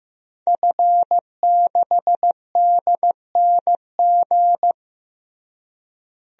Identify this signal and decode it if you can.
Morse code, heard as a keyed tone, about 15 wpm, 700 Hz. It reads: F6DNG
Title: DXCC-100 Call Signs - 5 Characters - Repeated 3x - 15wpm